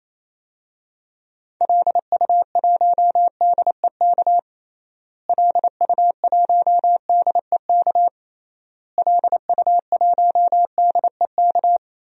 Morse code, heard as a keyed tone, about 28 wpm, 700 Hz. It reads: LU1DEK LU1DEK LU1DEK
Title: DXCC-100 Call Signs - 6 Characters - Repeated 3x - 28wpm